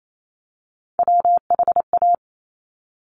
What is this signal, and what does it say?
Morse code, heard as a keyed tone, about 28 wpm, 700 Hz. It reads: WHA
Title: Top Three Letter Combinations in Words 28wpm